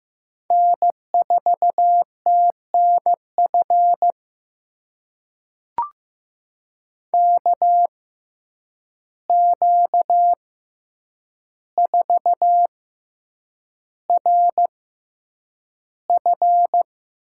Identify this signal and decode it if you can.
Morse code, heard as a keyed tone, about 15 wpm, spaced as for 8 wpm, 700 Hz. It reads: N4TNF KQ4RF
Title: USA Call Signs - 5 Characters - 6x Character Spacing - 15wpm